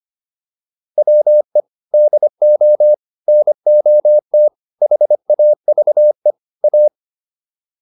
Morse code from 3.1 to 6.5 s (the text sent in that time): NOT HAVE